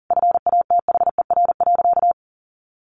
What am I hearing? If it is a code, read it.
Morse code, heard as a keyed tone, about 40 wpm, 700 Hz. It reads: FATHER.
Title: Sentences from Top 1300 Words 40wpm